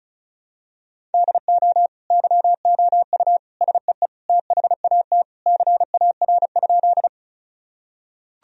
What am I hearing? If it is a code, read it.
Morse code, heard as a keyed tone, about 34 wpm, 700 Hz. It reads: DO YOU SEE THAT CAR?
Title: Sentences from Top 1000 Words - Review Part 1 - 35wpm